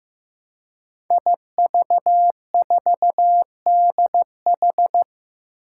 Morse code, heard as a keyed tone, about 15 wpm, 700 Hz. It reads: IV4DH
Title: Five Challenging Letter-Number Combinations 15wpm